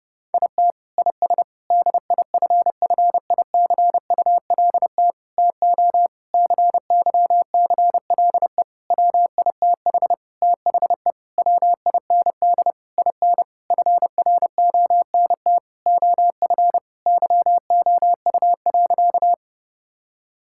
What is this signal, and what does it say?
Morse code, heard as a keyed tone, about 32 wpm, 700 Hz. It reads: IT IS DIFFICULT TO CYCLE WITH THE WIND IN FRONT OF YOU.
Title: Sentences from Top 1000 Words - Review Part 6 - 30wpm